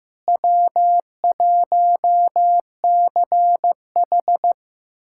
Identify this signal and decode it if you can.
Morse code, heard as a keyed tone, about 16 wpm, 700 Hz. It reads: W1CH
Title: USA Callsign DE Callsign - 4 Characters - 15wpm